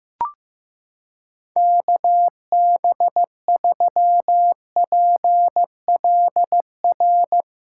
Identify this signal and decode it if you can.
Morse code, heard as a keyed tone, about 16 wpm, 700 Hz. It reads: KB3PLR
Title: USA Callsign DE Callsign 15wpm